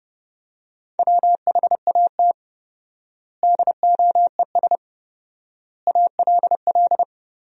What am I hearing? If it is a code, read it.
Morse code, heard as a keyed tone, about 30 wpm, 700 Hz. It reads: WHAT DOES ALL
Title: 4x Word-Spacing - Sentences from Top 200 Words - 30wpm